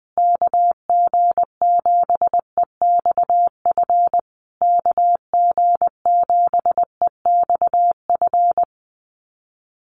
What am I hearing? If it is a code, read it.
Morse code, heard as a keyed tone, about 20 wpm, 700 Hz. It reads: KG7EXF KG7EXF
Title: USA Call Signs Repeated 3x 20wpm